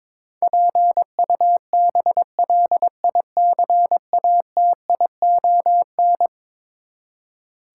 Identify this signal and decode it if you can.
Morse code, heard as a keyed tone, about 22 wpm, 700 Hz. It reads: PUBLICATION
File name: practice / 22wpm / Eleven Letter Words - 22wpm